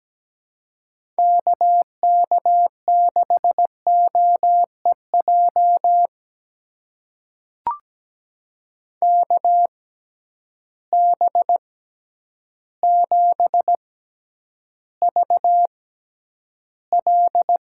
Morse code, heard as a keyed tone, about 17 wpm, spaced as for 9 wpm, 700 Hz. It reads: KK6OEJ KB7VL
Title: USA Call Signs - 6 Characters - 6x Character Spacing - 17wpm